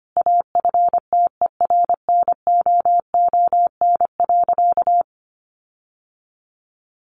Morse code, heard as a keyed tone, about 25 wpm, 700 Hz. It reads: AFTERNOON.
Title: Sentences from Top 1500 Words 25wpm